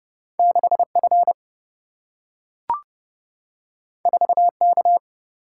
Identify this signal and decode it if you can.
Morse code, heard as a keyed tone, about 30 wpm, 700 Hz. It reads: 6F 4K
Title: Single-Letter Single-Number 30wpm